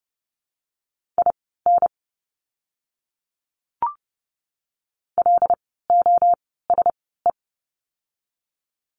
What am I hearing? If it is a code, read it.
Morse code, heard as a keyed tone, about 30 wpm, spaced as for 17 wpm, 700 Hz. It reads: IN LOSE
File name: Lesson 17f - Letter u - Words 30wpm